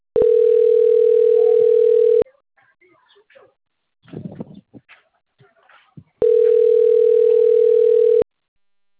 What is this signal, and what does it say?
Ringback tone (North America), 440 + 480 Hz.